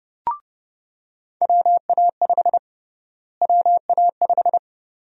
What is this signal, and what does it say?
Morse code, heard as a keyed tone, about 30 wpm, 700 Hz. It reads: WA5 WA5
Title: USA Call Sign Prefixes - Repeated 2x - 30wpm